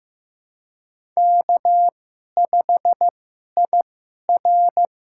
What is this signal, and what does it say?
Morse code, heard as a keyed tone, about 15 wpm, spaced as for 11 wpm, 700 Hz. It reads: K5IR
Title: USA Call Signs - 2x Character Spacing - 15wpm